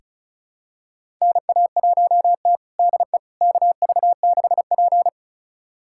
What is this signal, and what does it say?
Morse code, heard as a keyed tone, about 36 wpm, 700 Hz. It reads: NA1T DE KV6P
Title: USA Callsign DE Callsign - 4 Characters - 35wpm